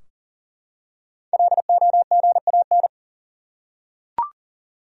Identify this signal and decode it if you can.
Morse code, heard as a keyed tone, about 37 wpm, 700 Hz. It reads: LOGAN
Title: USA Common Names - 40wpm